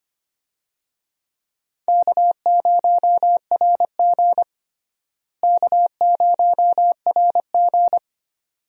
Morse code, heard as a keyed tone, about 25 wpm, 700 Hz. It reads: K0RG K0RG
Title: USA Call Signs - 4 Characters - Repeated 2x - 25wpm